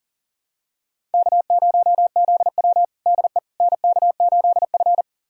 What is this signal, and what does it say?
Morse code, heard as a keyed tone, about 37 wpm, 700 Hz. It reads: K0ZW DE NK8F
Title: USA Callsign DE Callsign - 4 Characters - 40wpm